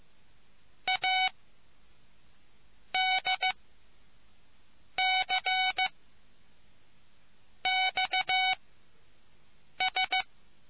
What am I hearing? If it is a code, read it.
Morse code, heard as a keyed tone, about 15 wpm, spaced as for 5 wpm, 750 Hz. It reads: ADCXS